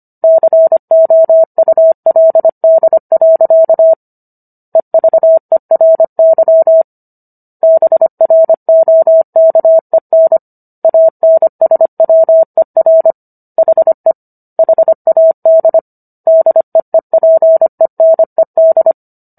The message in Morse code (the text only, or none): COULD. EVERY BROKEN ANSWER HE HAD DEEPENED